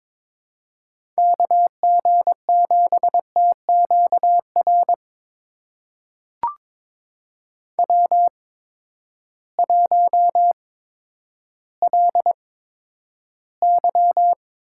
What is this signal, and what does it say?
Morse code, heard as a keyed tone, about 22 wpm, spaced as for 11 wpm, 700 Hz. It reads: KG7TQR W1LY
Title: USA Call Signs - 8x Character Spacing - 22wpm